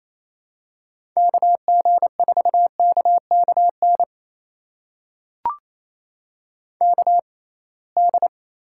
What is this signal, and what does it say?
Morse code, heard as a keyed tone, about 28 wpm, spaced as for 21 wpm, 700 Hz. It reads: KG4KKN KD
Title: USA Call Signs - 6 Characters - 6x Character Spacing - 28wpm